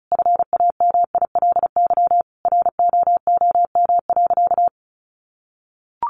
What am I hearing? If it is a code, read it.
Morse code, heard as a keyed tone, about 34 wpm, 700 Hz. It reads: FAMILY ROOM.
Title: Sentences from Top 1100 Words 35wpm